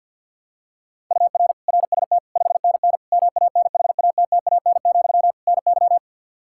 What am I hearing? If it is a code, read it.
Morse code, heard as a keyed tone, about 52 wpm, 700 Hz. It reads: UR RST 5NN MANHATTAN, NY
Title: Practice QSO - Exchange 1 - Beginning - 50wpm